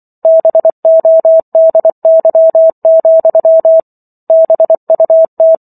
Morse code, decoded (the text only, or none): BODY, BUT